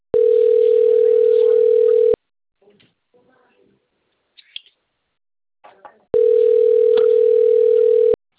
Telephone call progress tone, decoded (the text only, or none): ringback tone